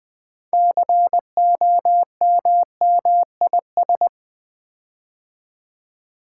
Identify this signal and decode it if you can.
Morse code, heard as a keyed tone, about 20 wpm, 700 Hz. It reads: COMMIS
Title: Build A Word - 6 Letters At A Time - 12-Letter Words - 20wpm